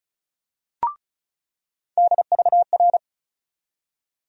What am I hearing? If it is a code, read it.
Morse code, heard as a keyed tone, about 36 wpm, 700 Hz. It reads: DVR
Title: USA Call Sign Postfixes - 3 Letters 35wpm